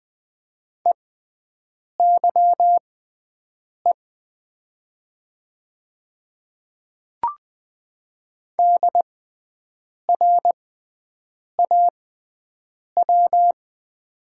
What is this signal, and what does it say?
Morse code, heard as a keyed tone, about 20 wpm, spaced as for 7 wpm, 700 Hz. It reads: EYE DRAW
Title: 6x Character Spacing - Top 300 Words - 20wpm